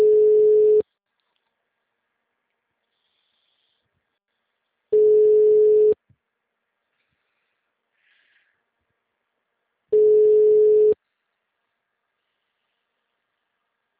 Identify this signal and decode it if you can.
Ringback tone (Europe, 425 Hz).
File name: out- -2007- - - .10434